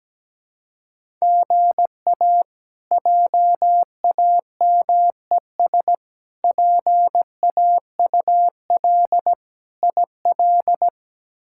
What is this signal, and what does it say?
Morse code, heard as a keyed tone, about 17 wpm, 700 Hz. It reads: GA JAMES PAUL IL
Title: K1USN SST QSO Practice - 17wpm